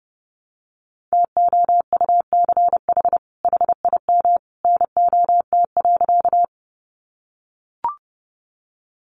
Morse code, heard as a keyed tone, about 30 wpm, 700 Hz. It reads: TOUCH HIM NOT.
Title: Sentences from Top 1200 Words 30wpm